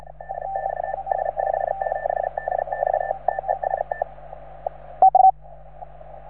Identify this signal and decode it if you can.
Morse code, heard as a keyed tone, about 36 wpm, 1750 Hz, with radio noise. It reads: DX S57DX TEST